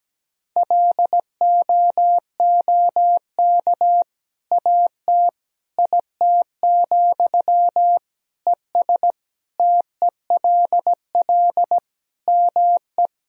Morse code, heard as a keyed tone, about 17 wpm, 700 Hz. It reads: LOOK AT IT, ES TELL ME
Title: Sentences from Top 200 Words and Abbreviations 17wpm